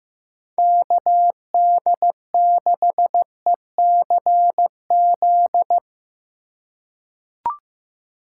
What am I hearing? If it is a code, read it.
Morse code, heard as a keyed tone, about 15 wpm, 700 Hz. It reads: KD6ECZ
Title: USA Call Signs - 6 Characters - 2x Character Spacing - 15wpm